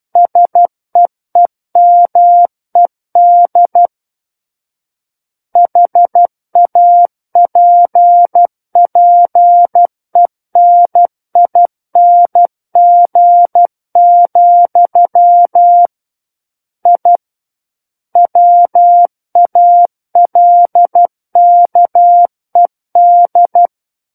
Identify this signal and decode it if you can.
Morse code, heard as a keyed tone, about 12 wpm, 700 Hz. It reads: SEEMED HAPPENING, I WALKED
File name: Book One / 03-Chapter 3 - 12 WPM